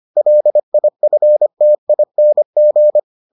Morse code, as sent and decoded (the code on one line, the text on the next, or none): .-.. .. ..-. - .. -. --.
LIFTING